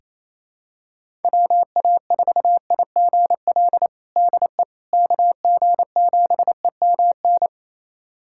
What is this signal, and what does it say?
Morse code, heard as a keyed tone, about 28 wpm, 700 Hz. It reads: WA4IGL DE KG7EMN
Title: USA Callsign DE Callsign - 6 Characters - 28wpm